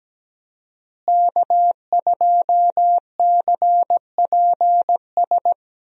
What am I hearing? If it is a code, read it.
Morse code, heard as a keyed tone, about 17 wpm, 700 Hz. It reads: K2CPS
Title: USA Call Signs - 5 Characters - Repeated 3x - 17wpm